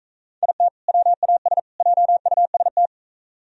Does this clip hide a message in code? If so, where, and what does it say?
Yes: Morse code from 0.4 to 2.9 s, heard as a keyed tone, about 42 wpm, 700 Hz. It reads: IT WAS JUST